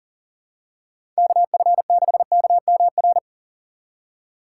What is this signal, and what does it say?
Morse code, heard as a keyed tone, about 40 wpm, 700 Hz. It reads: KF6KMR
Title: USA Call Signs - 6 Characters - 6x Character Spacing - 40wpm